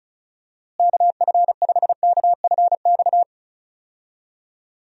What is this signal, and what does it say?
Morse code, heard as a keyed tone, about 36 wpm, 700 Hz. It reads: KF5KFX
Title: USA Call Signs - 6 Characters - Repeated 3x - 35wpm